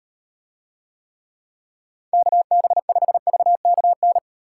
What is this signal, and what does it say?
Morse code, heard as a keyed tone, about 38 wpm, 700 Hz. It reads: KB5VKN